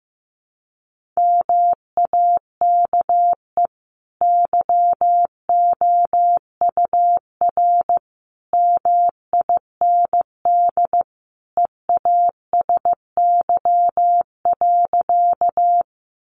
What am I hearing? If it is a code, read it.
Morse code, heard as a keyed tone, about 15 wpm, 700 Hz. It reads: MAKE YOUR MIND EASY.
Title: Sentences from Top 700 Words 15wpm